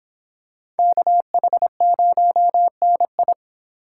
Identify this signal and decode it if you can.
Morse code, heard as a keyed tone, about 25 wpm, 700 Hz. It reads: KH0NI